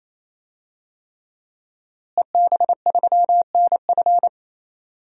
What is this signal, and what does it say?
Morse code, heard as a keyed tone, about 28 wpm, 700 Hz. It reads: EB3NF
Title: DXCC-100 Call Signs - 5 Characters - Repeated 3x - 28wpm